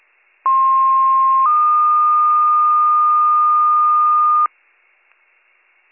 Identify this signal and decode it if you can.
Two-tone page, heard as a one-second tone, then a three-second tone, about 1060 then 1220 Hz, with noise. It is heard from 0.5 to 4.5 s.